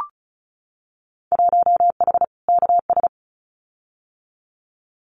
Morse code, heard as a keyed tone, about 36 wpm, 700 Hz. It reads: 1H KS
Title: Winter Field Day Exchanges 35wpm